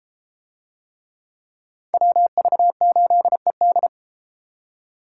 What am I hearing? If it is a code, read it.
Morse code, heard as a keyed tone, about 33 wpm, 700 Hz. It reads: WV8ED